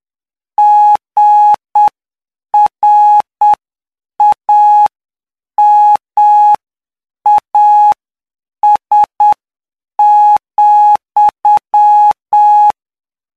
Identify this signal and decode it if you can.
Morse code, heard as a keyed tone, about 7 wpm, spaced as for 6 wpm, 840 Hz. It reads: GRAMAS,
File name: cw / py2lcd / 16-PY2LCD-Li��o 25